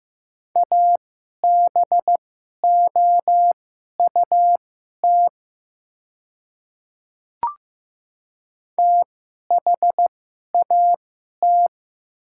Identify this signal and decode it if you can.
Morse code, heard as a keyed tone, about 15 wpm, spaced as for 11 wpm, 700 Hz. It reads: ABOUT THAT
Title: 2x Character Spacing - Top 100 Words - 15wpm